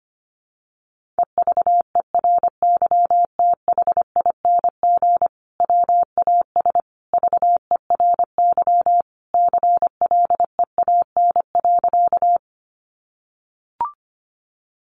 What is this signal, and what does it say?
Morse code, heard as a keyed tone, about 25 wpm, 700 Hz. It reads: EVERYTHING WAS VERY CLEAN.
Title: Sentences from Top 1100 Words 25wpm